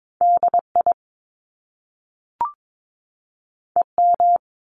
Morse code, heard as a keyed tone, about 22 wpm, 700 Hz. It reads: DI EM
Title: Top Two Letter Combinations in Words 22wpm